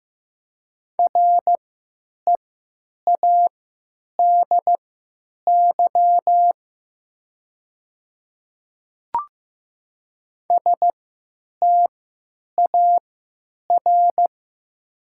Morse code, heard as a keyed tone, about 15 wpm, spaced as for 9 wpm, 700 Hz. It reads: READY STAR
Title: 3x Character Spacing - Top 500 Words - 15wpm